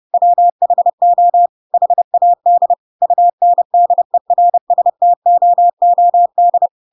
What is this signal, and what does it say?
Morse code, heard as a keyed tone, about 30 wpm, 700 Hz. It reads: WHO HAD UNDERSTOOD
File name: Pinocchio / Pinocchio ch7 30 30 ik1hge